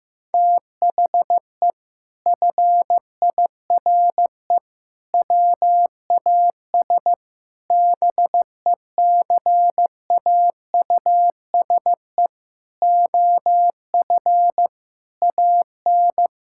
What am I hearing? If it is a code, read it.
Morse code, heard as a keyed tone, about 15 wpm, 700 Hz. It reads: THE FIRE WAS BECAUSE OF AN
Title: Sentences from Top 1000 Words - Review Part 6 - 15wpm